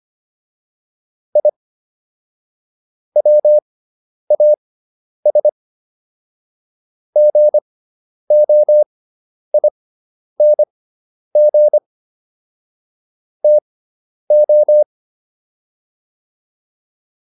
Morse code, heard as a keyed tone, about 25 wpm, spaced as for 10 wpm, 600 Hz. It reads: I WAS GOING TO
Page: intermediate-practice-files